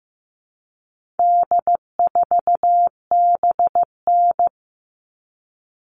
Morse code, heard as a keyed tone, about 15 wpm, 700 Hz. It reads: D4BN
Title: All Cut Numbers - 4-Digit 15wpm